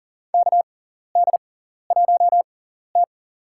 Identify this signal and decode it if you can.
Morse code, heard as a keyed tone, about 37 wpm, spaced as for 14 wpm, 700 Hz. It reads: KD1T
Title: USA Call Signs - 4 Characters - 6x Character Spacing - 40wpm